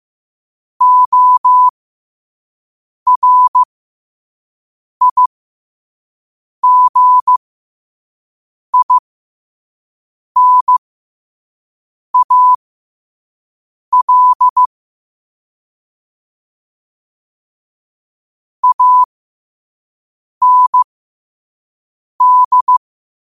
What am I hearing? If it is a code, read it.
Morse code, heard as a keyed tone, about 15 wpm, spaced as for 5 wpm, 1000 Hz. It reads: ORIGINAL AND